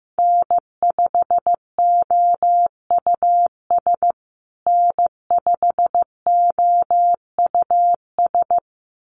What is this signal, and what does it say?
Morse code, heard as a keyed tone, about 15 wpm, 700 Hz. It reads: N5OUS N5OUS
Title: USA Call Signs Repeated 2x 15wpm